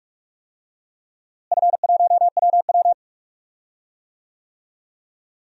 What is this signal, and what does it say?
Morse code, heard as a keyed tone, about 41 wpm, 700 Hz. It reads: F1WW
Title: DXCC-100 Call Signs - 4 Characters - Repeated 3x - 45wpm